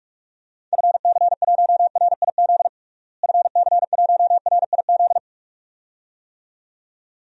Morse code, heard as a keyed tone, about 46 wpm, 700 Hz. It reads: FC1RIZ FC1RIZ
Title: DXCC-100 Call Signs - 6 Characters - Repeated 2x - 45wpm